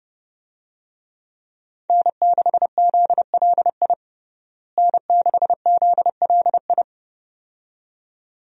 Morse code, heard as a keyed tone, about 30 wpm, 700 Hz. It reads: N6ZLI N6ZLI
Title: USA Call Signs - 5 Characters - Repeated 2x - 30wpm